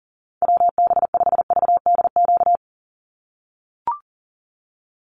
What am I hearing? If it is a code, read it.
Morse code, heard as a keyed tone, about 40 wpm, 700 Hz. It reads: WB5VDQ
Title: USA Call Signs 40wpm